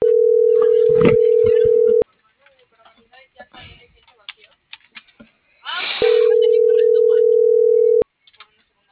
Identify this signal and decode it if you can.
Ringback tone (North America), 440 + 480 Hz.